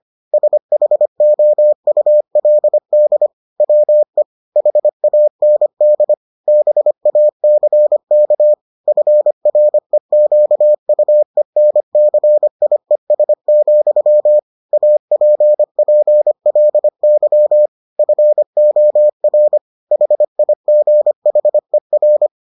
Morse code, heard as a keyed tone, about 25 wpm, 600 Hz. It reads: SHOULD WE HAND BACK FREQUENCIES, APPLY FOR HIGHER